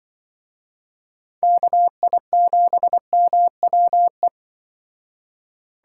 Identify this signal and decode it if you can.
Morse code, heard as a keyed tone, about 24 wpm, 700 Hz. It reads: KI7MWE
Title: Speed Racing - USA 2x3 Call Signs 20wpm